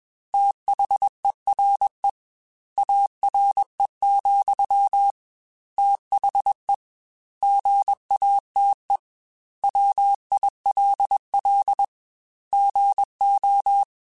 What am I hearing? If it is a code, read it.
Morse code, heard as a keyed tone, about 21 wpm, 800 Hz. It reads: THERE ARE, THE GATE WILL GO
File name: Technical Subjects / Technical Subjects-mp3s / OR Gate @20